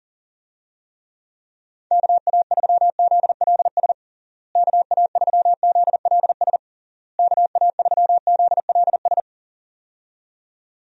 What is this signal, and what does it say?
Morse code, heard as a keyed tone, about 37 wpm, 700 Hz. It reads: KA3ZLS KA3ZLS KA3ZLS
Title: USA Call Signs - 6 Characters - Repeated 3x - 40wpm